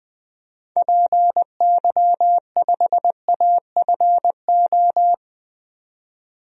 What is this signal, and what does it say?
Morse code, heard as a keyed tone, about 20 wpm, 700 Hz. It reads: PY5AFO
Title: DXCC-100 Call Signs - 6 Characters - 20wpm